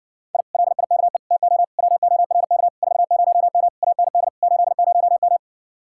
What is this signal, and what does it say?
Morse code, heard as a keyed tone, about 60 wpm, 700 Hz. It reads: I LIKE TO WORK 40M AND 80M